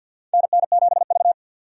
Morse code, heard as a keyed tone, about 45 wpm, 700 Hz. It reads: NN7V
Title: USA Call Signs - 4 Characters - 2x Character Spacing - 50wpm